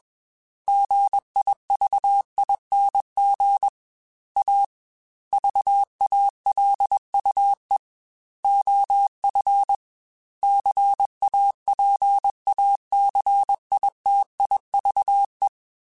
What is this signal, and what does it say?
Morse code, heard as a keyed tone, about 21 wpm, 800 Hz. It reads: GIVING A VALUE OF CAPACITIVE
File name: Technical Subjects / Technical Subjects-mp3s / XC @20